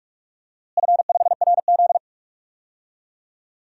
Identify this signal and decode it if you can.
Morse code, heard as a keyed tone, about 46 wpm, 700 Hz. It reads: F5RZ
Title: DXCC-100 Call Signs - 4 Characters - Repeated 3x - 45wpm